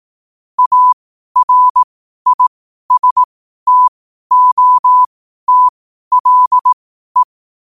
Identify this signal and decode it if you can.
Morse code, heard as a keyed tone, about 18 wpm, spaced as for 12 wpm, 1000 Hz. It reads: ARISTOTLE
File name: qotdfiles / 12wpm